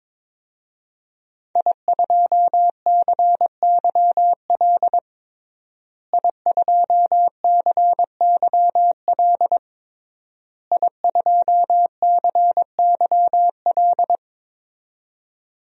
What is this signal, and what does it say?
Morse code, heard as a keyed tone, about 22 wpm, 700 Hz. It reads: I2CYL I2CYL I2CYL
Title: DXCC-100 Call Signs - 5 Characters - Repeated 3x - 22wpm